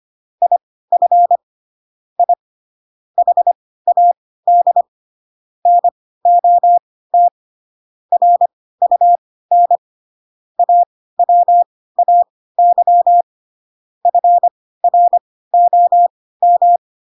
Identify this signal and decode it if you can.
Morse code, heard as a keyed tone, about 25 wpm, spaced as for 16 wpm, 700 Hz. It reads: IF I HAD NOT RUN AWAY FROM